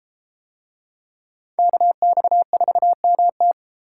Morse code, heard as a keyed tone, about 32 wpm, 700 Hz. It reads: KX4MT